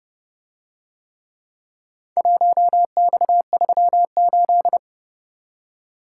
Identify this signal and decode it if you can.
Morse code, heard as a keyed tone, about 30 wpm, 700 Hz. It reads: 1X38